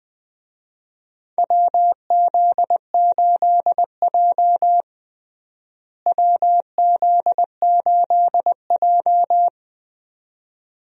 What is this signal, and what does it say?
Morse code, heard as a keyed tone, about 20 wpm, 700 Hz. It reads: WZ8J WZ8J
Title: USA Call Signs - 4 Characters - Repeated 2x - 20wpm